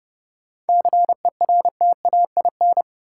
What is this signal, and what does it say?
Morse code, heard as a keyed tone, about 31 wpm, 700 Hz. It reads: CERTAIN
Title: Sentences from Top 500 Words - New Word Drill - 30wpm